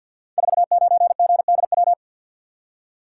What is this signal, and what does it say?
Morse code, heard as a keyed tone, about 47 wpm, 700 Hz. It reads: 49GDW
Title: Code Groups - 5 Letter:Numbers - 50wpm